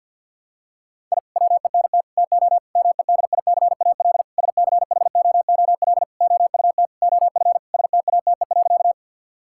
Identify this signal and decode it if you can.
Morse code, heard as a keyed tone, about 52 wpm, 690 Hz. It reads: I WENT TO MEDICAL SCHOOL OUT OF STATE.
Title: Sentences from Top 1000 Words - Review Part 5 - 50wpm